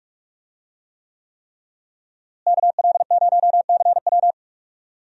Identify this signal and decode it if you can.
Morse code, heard as a keyed tone, about 46 wpm, 700 Hz. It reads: KL0CW
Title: USA Call Signs - 5 Characters - Repeated 3x - 45wpm